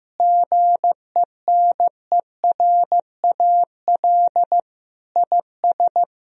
Morse code, heard as a keyed tone, about 16 wpm, 700 Hz. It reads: GENERAL IS
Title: Sentences from Top 1000 Words - Review Part 3 - 15wpm